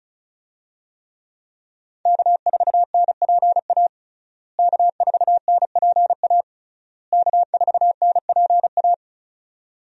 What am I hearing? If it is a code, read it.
Morse code, heard as a keyed tone, about 36 wpm, 700 Hz. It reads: K4NPA K4NPA K4NPA
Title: USA Call Signs - 5 Characters - Repeated 3x - 35wpm